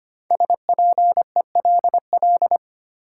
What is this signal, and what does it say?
Morse code, heard as a keyed tone, about 25 wpm, 700 Hz. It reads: SPELL
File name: 5 Letter Words - First 3 Letters Given - 25wpm